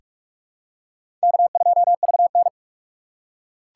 Morse code, heard as a keyed tone, about 47 wpm, 700 Hz. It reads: K2VN
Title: USA Call Signs - 4 Characters - 4x Character Spacing - 45wpm